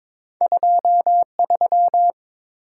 Morse code, heard as a keyed tone, about 32 wpm, spaced as for 27 wpm, 700 Hz. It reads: 23
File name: Speed-Racing - Two Letter-Number Combinations - 22wpm